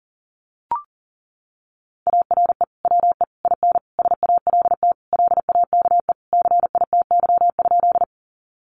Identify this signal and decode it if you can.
Morse code, heard as a keyed tone, about 40 wpm, 700 Hz. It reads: ARE WE IN SALT LAKE CITY?
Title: Sentences from Top 1100 Words 40wpm